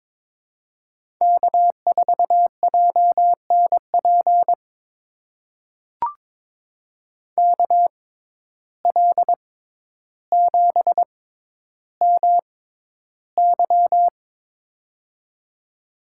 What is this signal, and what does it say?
Morse code, heard as a keyed tone, about 21 wpm, spaced as for 11 wpm, 700 Hz. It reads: K4JNP KL7MY
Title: USA Call Signs - 5 Characters - 6x Character Spacing - 22wpm